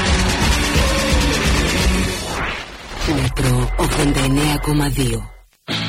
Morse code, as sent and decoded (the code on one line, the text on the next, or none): none